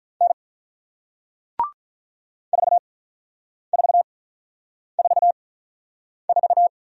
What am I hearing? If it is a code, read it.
Morse code, heard as a keyed tone, about 47 wpm, spaced as for 9 wpm, 700 Hz. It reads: N 4444